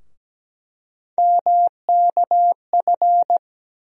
Morse code, heard as a keyed tone, about 17 wpm, 700 Hz. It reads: MKF
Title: USA Call Sign Postfixes - 3 Letters 17wpm